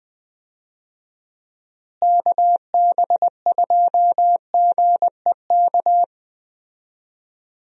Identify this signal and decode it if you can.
Morse code, heard as a keyed tone, about 20 wpm, 700 Hz. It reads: KB2GEK
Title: USA Call Signs - 6 Characters - Repeated 2x - 20wpm